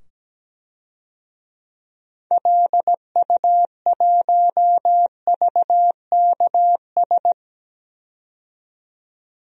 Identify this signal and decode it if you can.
Morse code, heard as a keyed tone, about 17 wpm, 700 Hz. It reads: LU1VKS
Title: DXCC-100 Call Signs - 6 Characters - 17wpm